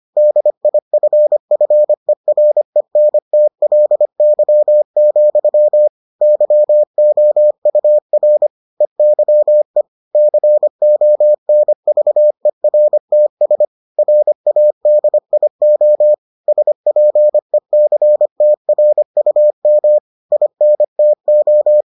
Morse code, heard as a keyed tone, about 25 wpm, 600 Hz. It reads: DIFFERENTLY, YOUR EYE CONVERTS RADIO SPECTRUM INTO